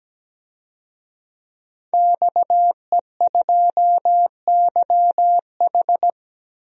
Morse code, heard as a keyed tone, about 17 wpm, 700 Hz. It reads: XE2YH
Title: DXCC-100 Call Signs - 5 Characters - 17wpm